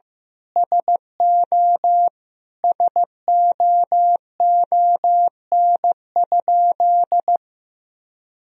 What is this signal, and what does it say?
Morse code, heard as a keyed tone, about 16 wpm, 700 Hz. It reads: SO SOON?
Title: Sentences from Top 400 Words and Abbreviations 15wpm